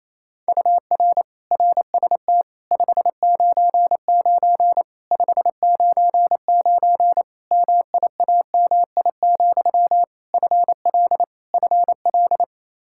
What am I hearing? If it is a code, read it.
Morse code, heard as a keyed tone, about 28 wpm, 700 Hz. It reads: UR RST 599 599 MIAMI, FL FL
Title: Practice QSO - Exchange 1 - Beginning - 28wpm